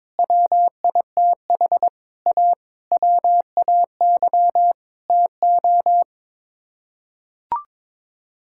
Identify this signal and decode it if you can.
Morse code, heard as a keyed tone, about 22 wpm, 700 Hz. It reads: WITH A WAY TO
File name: practice / 22wpm / Sets of 7 Words - 22wpm